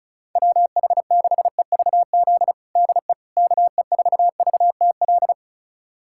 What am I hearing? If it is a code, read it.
Morse code, heard as a keyed tone, about 36 wpm, 700 Hz. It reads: WH6EVZ DE KE4VTL
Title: USA Callsign DE Callsign - 6 Characters - 35wpm